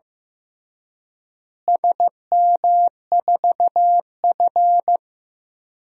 Morse code, heard as a keyed tone, about 15 wpm, 700 Hz. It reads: SM4F
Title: DXCC-100 Call Signs - 4 Characters - Repeated 2x - 15wpm